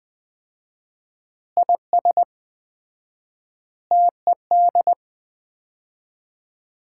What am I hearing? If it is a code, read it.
Morse code, heard as a keyed tone, about 20 wpm, 700 Hz. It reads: IS TED
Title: Common QSO Phrases - 4x Word-Spacing - 20wpm